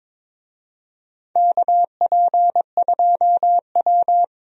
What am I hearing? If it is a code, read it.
Morse code, heard as a keyed tone, about 22 wpm, 700 Hz. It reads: KP2W
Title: USA Call Signs - 4 Characters - 22wpm